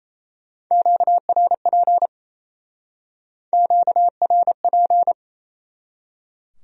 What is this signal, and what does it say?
Morse code, heard as a keyed tone, about 36 wpm, 700 Hz. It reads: QRP QRP